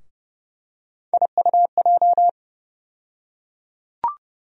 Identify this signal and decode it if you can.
Morse code, heard as a keyed tone, about 30 wpm, 700 Hz. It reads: IUJ
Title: USA Call Sign Postfixes - 3 Letters 30wpm